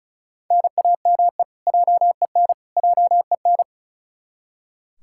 Morse code, heard as a keyed tone, about 34 wpm, 700 Hz. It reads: NAME JEN JEN
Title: K1USN SST QSO Practice 35wpm